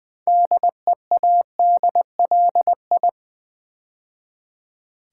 Morse code, heard as a keyed tone, about 20 wpm, 700 Hz. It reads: DEADLI
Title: Build A Word - 2 Letters At A Time - 8-Letter Words - 20wpm